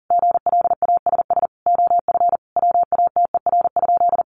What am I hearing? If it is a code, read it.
Morse code, heard as a keyed tone, about 40 wpm, 700 Hz. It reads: GLASS OF WATER?
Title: Sentences from Top 1300 Words 40wpm